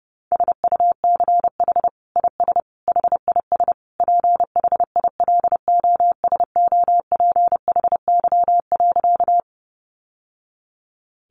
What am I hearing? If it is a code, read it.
Morse code, heard as a keyed tone, about 30 wpm, 700 Hz. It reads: SUCH IS HIS PHILOSOPHY.
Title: Sentences from Top 900 Words 30wpm